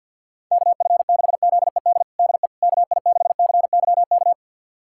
Morse code, heard as a keyed tone, about 52 wpm, 700 Hz. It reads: KF6ZED DE KI6CYK
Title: USA Callsign DE Callsign - 6 Characters - 50wpm